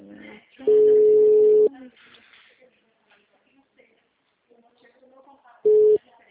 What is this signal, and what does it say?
Ringback tone (Europe, 425 Hz), with speech.
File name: out- -2014- - - .1466